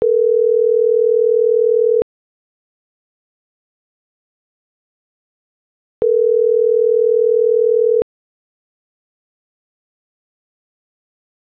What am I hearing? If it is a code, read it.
Ringback tone (North America), 440 + 480 Hz.